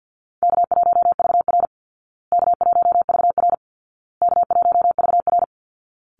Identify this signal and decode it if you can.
Morse code, heard as a keyed tone, about 50 wpm, 700 Hz. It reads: K1VR K1VR K1VR
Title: Call Signs Repeated 2x - 50wpm